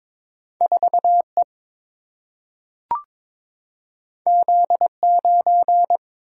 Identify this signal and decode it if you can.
Morse code, heard as a keyed tone, about 22 wpm, 700 Hz. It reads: 4 E Z 9
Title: Single-Letter Single-Number 22wpm